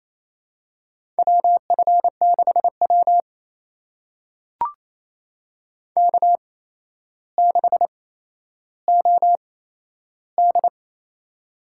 Morse code, heard as a keyed tone, about 29 wpm, spaced as for 12 wpm, 700 Hz. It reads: WF6W K6OD
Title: USA Call Signs - 4 Characters - 8x Character Spacing - 28wpm